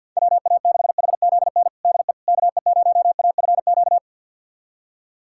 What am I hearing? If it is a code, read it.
Morse code, heard as a keyed tone, about 51 wpm, 690 Hz. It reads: WA6HZN DE KE0AFQ
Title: USA Callsign DE Callsign - 6 Characters - 50wpm